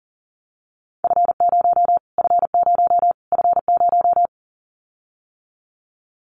Morse code, heard as a keyed tone, about 40 wpm, 700 Hz. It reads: F0 F0 F0
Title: Callsign Prefixes for Top 25 DXCC - Repeated 3x - 40wpm